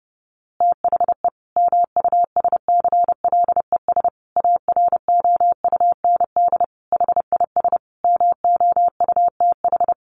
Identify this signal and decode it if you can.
Morse code, heard as a keyed tone, about 30 wpm, 700 Hz. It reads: THE MUSCLES AROUND HIS MOUTH
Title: A Sound Of Thunder 30wpm